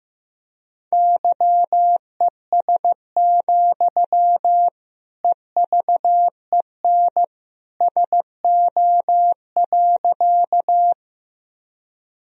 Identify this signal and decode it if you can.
Morse code, heard as a keyed tone, about 15 wpm, 700 Hz. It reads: YES, EVEN SO.
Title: Sentences from Top 1000 Words - Review Part 1 - 15wpm